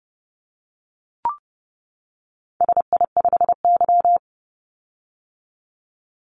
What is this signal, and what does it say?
Morse code, heard as a keyed tone, about 30 wpm, 700 Hz. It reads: SI5Y
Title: Call Signs - 30wpm